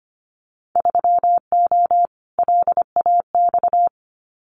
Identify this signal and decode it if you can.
Morse code, heard as a keyed tone, about 25 wpm, 700 Hz. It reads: 3O LAX
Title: Winter Field Day Exchanges Encore 25wpm